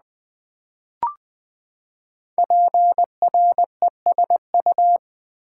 Morse code, heard as a keyed tone, about 20 wpm, 700 Hz. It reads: PRESU
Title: Build A Word - 5 Letters At A Time - 10-Letter Words - 20wpm